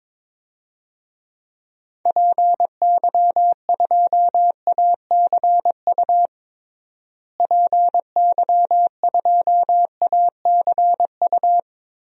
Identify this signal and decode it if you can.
Morse code, heard as a keyed tone, about 21 wpm, 700 Hz. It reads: PY2ACU PY2ACU
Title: DXCC-100 Call Signs - 6 Characters - Repeated 3x - 22wpm